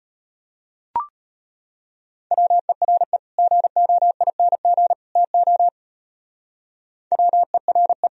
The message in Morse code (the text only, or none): WERE GOING TO WERE